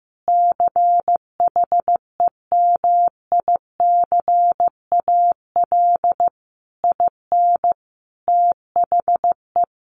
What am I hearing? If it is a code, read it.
Morse code, heard as a keyed tone, about 15 wpm, 700 Hz. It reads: CHEMICAL IN THE
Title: Sentences from Top 1200 Words 15wpm